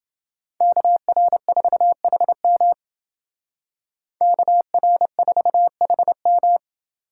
Morse code, heard as a keyed tone, about 28 wpm, 700 Hz. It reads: KR4HM KR4HM